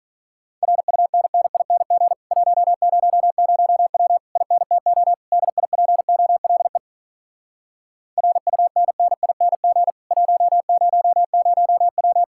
Morse code, heard as a keyed tone, about 46 wpm, 700 Hz. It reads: RUNNING 100W INTO DIPOLE RUNNING 100W